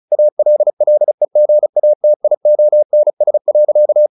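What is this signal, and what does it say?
Morse code, heard as a keyed tone, about 34 wpm, 600 Hz. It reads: ALLEGATIONS.
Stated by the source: Morse practice at 35 wpm from CW med Gnister, sending Ken Follett's novel Never in English.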